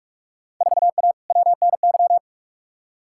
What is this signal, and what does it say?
Morse code, heard as a keyed tone, about 44 wpm, 700 Hz. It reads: 4A WNY
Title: Field Day Exchanges Encore - 45wpm